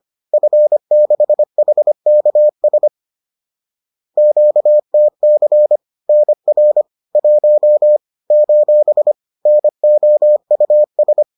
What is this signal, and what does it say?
Morse code, heard as a keyed tone, about 25 wpm, 600 Hz. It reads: F6HKS QTC NR 1 : NOUS
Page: bulletin UFT sous forme texte et en CW